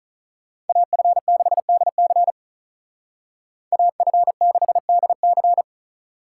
Morse code, heard as a keyed tone, about 43 wpm, 700 Hz. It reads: AF6DC AF6DC
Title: Speed Racing - USA 2x2 Call Signs 35wpm